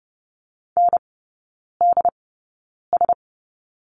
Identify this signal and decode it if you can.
Morse code, heard as a keyed tone, about 30 wpm, spaced as for 9 wpm, 690 Hz. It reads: NDS
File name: Lesson 17e - Letter u - Three Characters 30wpm